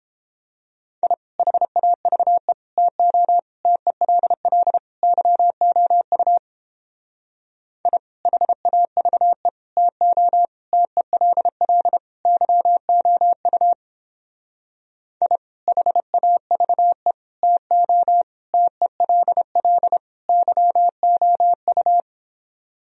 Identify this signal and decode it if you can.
Morse code, heard as a keyed tone, about 29 wpm, 700 Hz. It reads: I HAVE TO TELL YOU I HAVE TO TELL YOU I HAVE TO TELL YOU